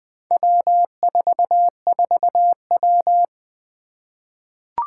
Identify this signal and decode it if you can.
Morse code, heard as a keyed tone, about 20 wpm, 700 Hz. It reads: W44W
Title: Four Challenging Letter-Number Combinations 20wpm